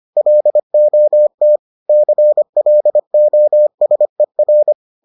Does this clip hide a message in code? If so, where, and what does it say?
Yes: Morse code from 0.2 to 4.7 s, heard as a keyed tone, about 25 wpm, 600 Hz. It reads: LOT CLOSER